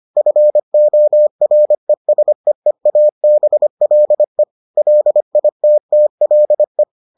Morse code, heard as a keyed tone, about 25 wpm, 600 Hz. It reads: FORESEEABLE LITTLE